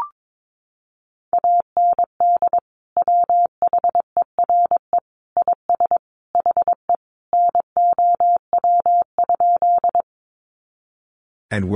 Morse code, heard as a keyed tone, about 22 wpm, 700 Hz. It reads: AND WHERE IS HE NOW?
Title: Sentences from Top 200 Words 22wpm